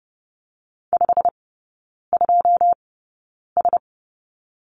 Morse code, heard as a keyed tone, about 30 wpm, spaced as for 9 wpm, 700 Hz. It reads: 52S
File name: course / 30:10wpm / 087 Lesson 15e - Number 5 - Three Characters 30wpm